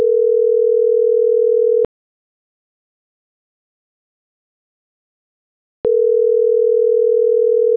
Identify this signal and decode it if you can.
Ringback tone (North America), 440 + 480 Hz.